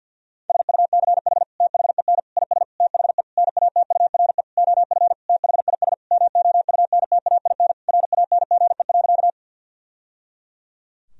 Morse code, heard as a keyed tone, about 50 wpm, 700 Hz. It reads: SUCH THEN IS THE NATURE OF THIS MOUNTAIN RANGE.
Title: Sentences from Top 1000 Words - Review Part 4 - 50wpm